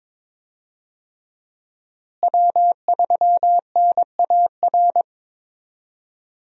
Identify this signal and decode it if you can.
Morse code, heard as a keyed tone, about 22 wpm, 700 Hz. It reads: W3NAR
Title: USA Call Signs - 5 Characters - Repeated 3x - 22wpm